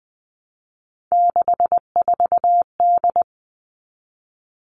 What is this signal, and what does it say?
Morse code, heard as a keyed tone, about 20 wpm, 700 Hz. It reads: 64D
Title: Three Challenging Letter-Number Combinations 20wpm